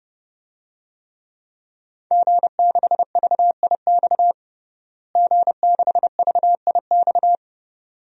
Morse code, heard as a keyed tone, about 30 wpm, 700 Hz. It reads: G6VIX G6VIX
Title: DXCC-100 Call Signs - 5 Characters - Repeated 2x - 30wpm